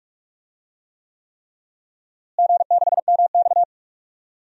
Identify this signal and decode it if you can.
Morse code, heard as a keyed tone, about 46 wpm, 700 Hz. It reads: G6MX
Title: DXCC-100 Call Signs - 4 Characters - Repeated 3x - 45wpm